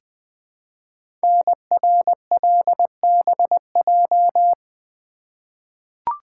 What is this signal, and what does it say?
Morse code, heard as a keyed tone, about 20 wpm, 700 Hz. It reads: NRLBJ
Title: Code Groups - 5 Letters - 20wpm